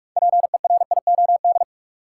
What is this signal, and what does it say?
Morse code, heard as a keyed tone, about 45 wpm, 700 Hz. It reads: PERIOD
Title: Six Letter Words - 45wpm